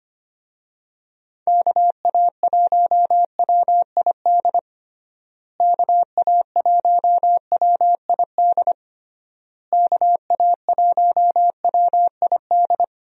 Morse code, heard as a keyed tone, about 25 wpm, 700 Hz. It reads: KA1WID KA1WID KA1WID
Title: USA Call Signs - 6 Characters - Repeated 3x - 25wpm